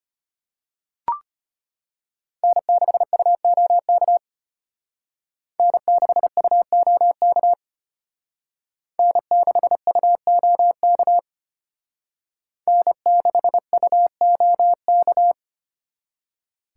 Morse code, heard as a keyed tone, about 33 wpm, 700 Hz. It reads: N6UOK N6UOK N6UOK N6UOK